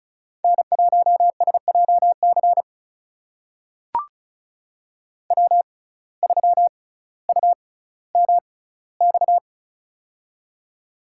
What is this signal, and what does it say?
Morse code, heard as a keyed tone, about 36 wpm, spaced as for 18 wpm, 700 Hz. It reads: N1SJC W3UMX
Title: USA Call Signs - 5 Characters - 6x Character Spacing - 35wpm